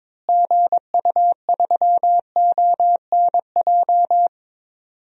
Morse code, heard as a keyed tone, about 22 wpm, 700 Hz. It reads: GU3ONJ
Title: DXCC-100 Call Signs - 6 Characters - Repeated 2x - 22wpm